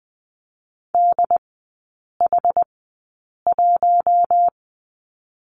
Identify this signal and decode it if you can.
Morse code, heard as a keyed tone, about 19 wpm, spaced as for 8 wpm, 700 Hz. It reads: DH1
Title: 067 Lesson 12e - Letter d - Three Characters 20wpm